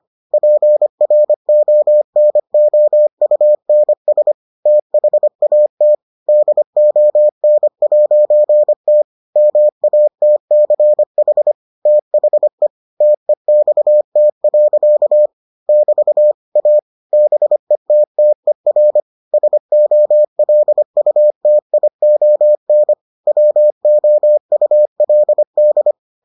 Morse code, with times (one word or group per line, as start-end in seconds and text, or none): 0.3-4.3 PRONOUNS
4.7-5.9 THAT
6.3-9.0 DON'T
9.4-11.5 MATCH
11.9-12.7 THE
13.0-15.3 TEXT.
15.7-16.3 <BT>
16.6-16.8 A
17.1-19.0 BETTER
19.3-22.9 SOLUTION
23.3-25.9 WOULD